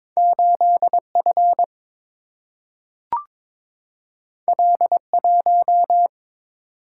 Morse code, heard as a keyed tone, about 22 wpm, 700 Hz. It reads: 8 F L 1
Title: Single-Letter Single-Number 22wpm